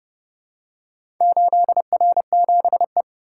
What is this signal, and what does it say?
Morse code, heard as a keyed tone, about 30 wpm, 700 Hz. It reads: 8R7E